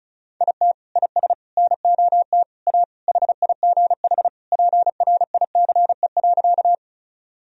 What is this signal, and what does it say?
Morse code, heard as a keyed tone, about 34 wpm, 700 Hz. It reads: IT IS NOT A HIGH PRICE.
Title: Sentences from Top 1000 Words - Review Part 4 - 35wpm